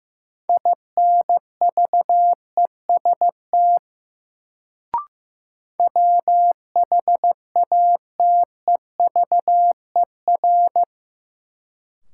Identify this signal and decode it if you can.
Morse code, heard as a keyed tone, about 15 wpm, 700 Hz. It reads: INVEST WHATEVER
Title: Sentences from Top 1100 Words - New Word Drill Alt - 15wpm